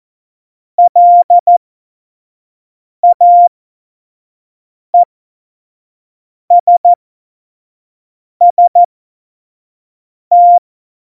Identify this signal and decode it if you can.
Morse code, heard as a keyed tone, about 14 wpm, spaced as for 5 wpm, 700 Hz. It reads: LAESST
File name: Morsekurs / Witze / Witz 05 Mai 2020, 25 F70